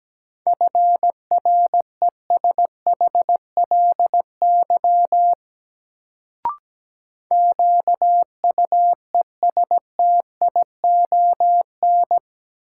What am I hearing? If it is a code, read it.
Morse code, heard as a keyed tone, about 17 wpm, 700 Hz. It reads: FRESHLY QUESTION
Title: Sets of 5 Words Encore - New Word Drill - 17wpm